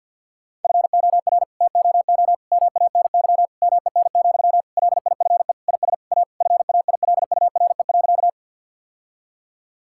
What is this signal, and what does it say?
Morse code, heard as a keyed tone, about 50 wpm, 700 Hz. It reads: FOR TOO MANY MEN, LIFE IS A FAILURE.
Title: Sentences from Top 1000 Words - Review Part 6 - 50wpm